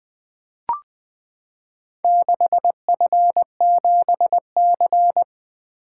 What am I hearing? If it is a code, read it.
Morse code, heard as a keyed tone, about 20 wpm, 700 Hz. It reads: 6F7C
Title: Four Letter-Number Combinations - 20wpm